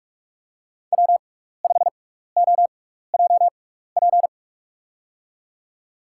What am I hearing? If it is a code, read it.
Morse code, heard as a keyed tone, about 46 wpm, spaced as for 16 wpm, 700 Hz. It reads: W5OJP
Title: USA Call Signs - 5 Characters - 6x Character Spacing - 45wpm